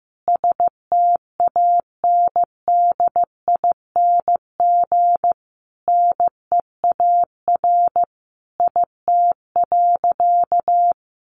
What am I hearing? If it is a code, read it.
Morse code, heard as a keyed tone, about 15 wpm, 700 Hz. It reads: STANDING NEAR IT.
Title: A Sound Of Thunder 15wpm